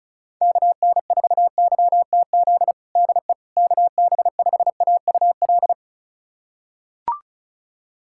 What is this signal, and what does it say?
Morse code, heard as a keyed tone, about 34 wpm, 700 Hz. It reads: KN4YTZ DE KB5AUL
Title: USA Callsign DE Callsign - 6 Characters - 35wpm